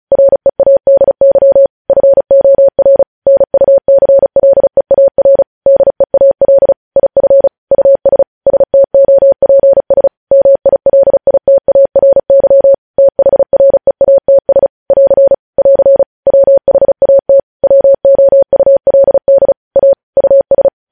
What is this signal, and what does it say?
Morse code, heard as a keyed tone, about 36 wpm, 550 Hz. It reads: READY FOR NUCLEAR DEAL IF US STOPS MILITARY THREATS <AR> <AR> WHAT WOULD A US
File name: podcast / News headlines-1x-35x35